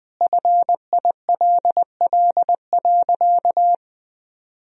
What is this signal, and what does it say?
Morse code, heard as a keyed tone, about 20 wpm, 700 Hz. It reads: FILL.
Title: Sentences from Top 1000 Words - Review Part 3 - 20wpm